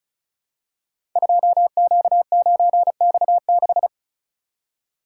Morse code, heard as a keyed tone, about 36 wpm, 690 Hz. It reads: 2Q9X6
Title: Code Groups - 5 Letter:Numbers - 35wpm